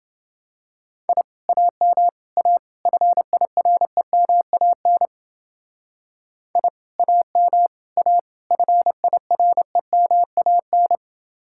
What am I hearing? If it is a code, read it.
Morse code, heard as a keyed tone, about 29 wpm, 700 Hz. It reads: I AM A FIREMAN I AM A FIREMAN